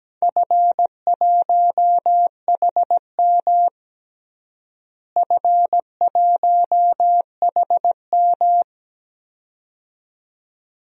Morse code, heard as a keyed tone, about 17 wpm, 700 Hz. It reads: F1HM F1HM
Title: DXCC-100 Call Signs - 4 Characters - Repeated 2x - 17wpm